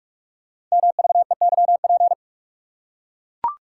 Morse code, heard as a keyed tone, about 44 wpm, 700 Hz. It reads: MVEYP
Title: Code Groups - 5 Letters - 45wpm